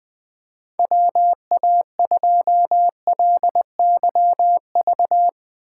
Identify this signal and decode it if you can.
Morse code, heard as a keyed tone, about 21 wpm, 700 Hz. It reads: WA2LYV
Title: USA Call Signs - 6 Characters - 8x Character Spacing - 20wpm